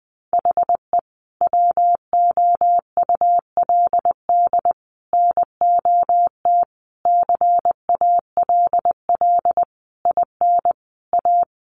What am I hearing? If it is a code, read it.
Morse code, heard as a keyed tone, about 20 wpm, 700 Hz. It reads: HE WOULD NOT CALL IN A
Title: Sentences from Top 1500 Words 20wpm